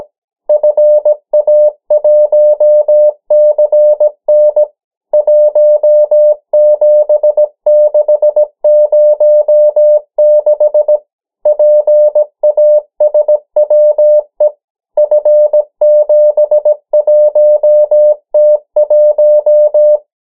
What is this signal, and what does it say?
Morse code, heard as a keyed tone, about 17 wpm, 600 Hz. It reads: FA1CN 17606 PASWE F71T1
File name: morse mp3 / 14 urok 13